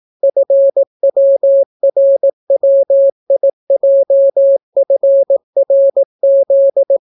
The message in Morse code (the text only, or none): FWRWIJFRZ